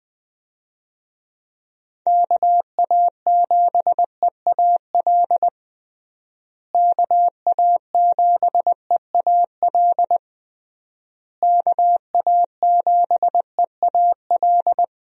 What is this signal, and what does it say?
Morse code, heard as a keyed tone, about 20 wpm, 700 Hz. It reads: KA7EAL KA7EAL KA7EAL
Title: USA Call Signs - 6 Characters - Repeated 3x - 20wpm